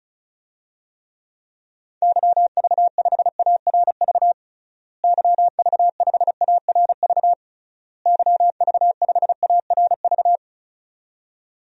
Morse code, heard as a keyed tone, about 36 wpm, 700 Hz. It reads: YV5ARV YV5ARV YV5ARV
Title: DXCC-100 Call Signs - 6 Characters - Repeated 3x - 35wpm